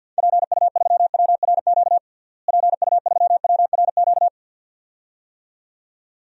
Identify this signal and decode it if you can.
Morse code, heard as a keyed tone, about 47 wpm, 700 Hz. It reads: PU3WRQ PU3WRQ
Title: DXCC-100 Call Signs - 6 Characters - Repeated 2x - 50wpm